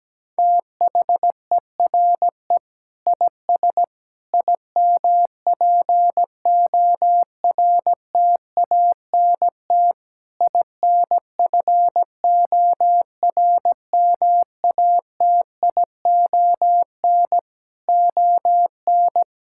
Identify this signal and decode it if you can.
Morse code, heard as a keyed tone, about 17 wpm, 700 Hz. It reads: THERE IS IMPORTANT INFORMATION ON
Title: Sentences from Top 1000 Words - Review Part 6 - 17wpm